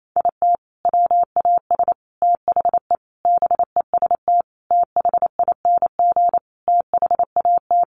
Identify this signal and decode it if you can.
Morse code, heard as a keyed tone, about 28 wpm, 700 Hz. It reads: IT WAS THE BEST THING THAT
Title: Sentences from Top 400 Words 28wpm